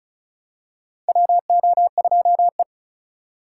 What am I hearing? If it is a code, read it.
Morse code, heard as a keyed tone, about 34 wpm, 700 Hz. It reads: WO2E
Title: K1USN SST QSO Practice - 35wpm